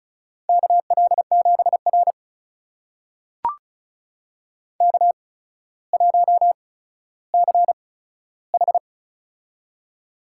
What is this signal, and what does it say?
Morse code, heard as a keyed tone, about 36 wpm, spaced as for 15 wpm, 700 Hz. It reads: KL7R K1CH
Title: USA Call Signs - 4 Characters - 8x Character Spacing - 35wpm